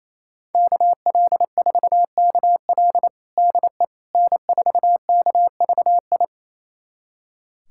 Morse code, heard as a keyed tone, about 28 wpm, 700 Hz. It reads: KL4KL DE N4KVI
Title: USA Callsign DE Callsign - 5 Characters - 28wpm